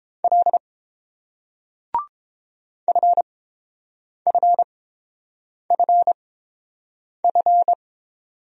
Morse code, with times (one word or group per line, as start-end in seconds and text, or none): 0.2-0.6 L
2.9-7.7 FFFF